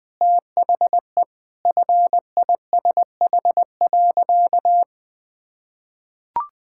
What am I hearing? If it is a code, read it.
Morse code, heard as a keyed tone, about 20 wpm, 700 Hz. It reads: THE FISH.
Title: Sentences with 8-Word Phrases - 20wpm